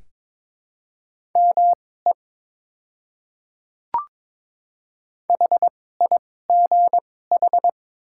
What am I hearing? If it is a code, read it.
Morse code, heard as a keyed tone, about 22 wpm, spaced as for 16 wpm, 690 Hz. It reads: ME HIGH
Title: 2x Character Spacing - Top 300 Words - 22wpm